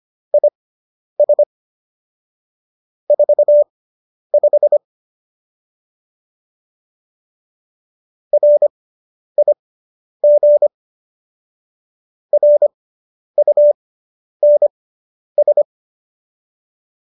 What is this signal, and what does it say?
Morse code, heard as a keyed tone, about 25 wpm, spaced as for 10 wpm, 590 Hz. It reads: IS 45 RIG RUNS
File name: QSO220 10